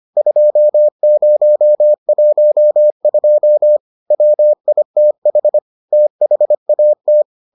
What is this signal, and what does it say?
Morse code, heard as a keyed tone, about 25 wpm, 600 Hz. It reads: 2012 WITH THAT